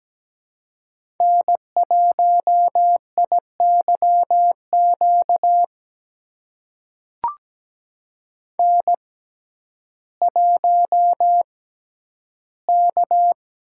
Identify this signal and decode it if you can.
Morse code, heard as a keyed tone, about 17 wpm, spaced as for 10 wpm, 700 Hz. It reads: N1IYQ N1K
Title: USA Call Signs - 6x Character Spacing - 17wpm